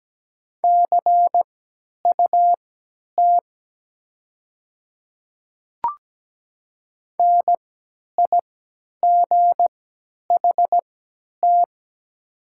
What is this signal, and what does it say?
Morse code, heard as a keyed tone, about 17 wpm, spaced as for 10 wpm, 700 Hz. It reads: CUT NIGHT
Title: 3x Character Spacing - Top 500 Words - 17wpm